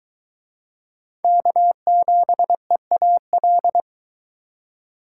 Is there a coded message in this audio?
Yes: Morse code from 1.2 to 3.8 s, heard as a keyed tone, about 23 wpm, 700 Hz. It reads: K7EAL